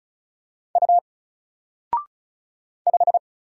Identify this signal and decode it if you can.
Morse code, heard as a keyed tone, about 34 wpm, 700 Hz. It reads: U 5
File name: Single-Letter-Number-35wpm